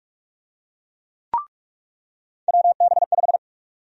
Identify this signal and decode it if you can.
Morse code, heard as a keyed tone, about 47 wpm, 700 Hz. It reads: WB5
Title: USA Call Sign Prefixes - 45wpm